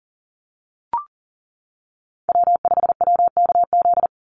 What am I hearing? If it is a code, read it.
Morse code, heard as a keyed tone, about 40 wpm, 700 Hz. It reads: W5WKZ
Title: USA Call Signs 40wpm